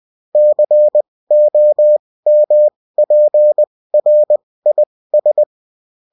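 Morse code, heard as a keyed tone, about 20 wpm, spaced as for 16 wpm, 600 Hz. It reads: COMPRIS
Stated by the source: Bulletin in text and CW form.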